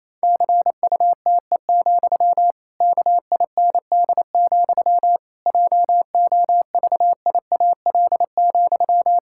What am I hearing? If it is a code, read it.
Morse code, heard as a keyed tone, about 28 wpm, 700 Hz. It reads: CUTE, KIND, JOVIAL,
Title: Pangram Challenge v2 - 28wpm